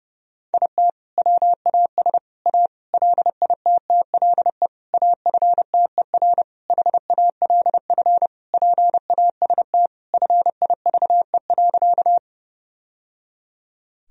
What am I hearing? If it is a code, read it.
Morse code, heard as a keyed tone, about 30 wpm, 700 Hz. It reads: IT WAS A LITTLE AFTER HALF PAST FIVE.
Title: Sentences from Top 1000 Words - Review Part 3 - 30wpm